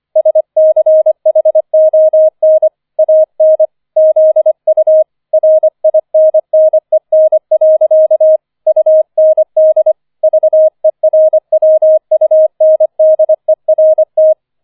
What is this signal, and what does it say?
Morse code, heard as a keyed tone, about 25 wpm, 630 Hz. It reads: SCHON AN ZU RINNEN. UND VERWUNDERT